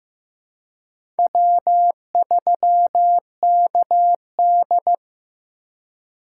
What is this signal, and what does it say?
Morse code, heard as a keyed tone, about 16 wpm, 700 Hz. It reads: W3KD
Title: USA Call Signs - 4 Characters - 3x Character Spacing - 15wpm